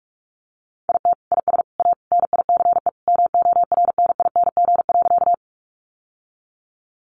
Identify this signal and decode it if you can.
Morse code, heard as a keyed tone, about 44 wpm, 700 Hz. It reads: IT IS A NICE MORNING.
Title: Sentences from Top 1100 Words 45wpm